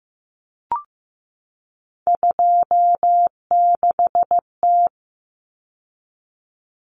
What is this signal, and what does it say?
Morse code, heard as a keyed tone, about 15 wpm, 700 Hz. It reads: 26T
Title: Common Cut Numbers - 3-Digit 15wpm